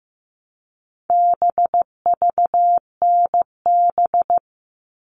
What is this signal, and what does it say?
Morse code, heard as a keyed tone, about 15 wpm, 700 Hz. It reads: BVNB
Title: All Cut Numbers - 4-Digit 15wpm